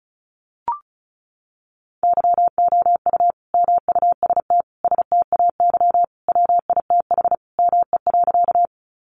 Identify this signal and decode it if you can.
Morse code, heard as a keyed tone, about 34 wpm, 700 Hz. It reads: YOU MUST STAY WITH ME.
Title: Sentences from Top 500 Words 35wpm